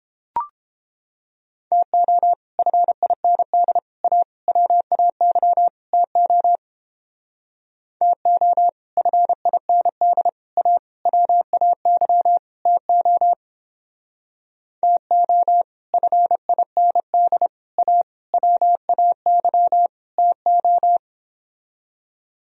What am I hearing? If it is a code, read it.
Morse code, heard as a keyed tone, about 29 wpm, 700 Hz. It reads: TO FIND A WAY TO TO FIND A WAY TO TO FIND A WAY TO